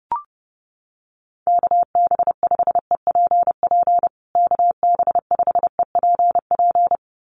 Morse code, heard as a keyed tone, about 30 wpm, 700 Hz. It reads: KB5EPP KB5EPP
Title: USA Call Signs Repeated 2x 30wpm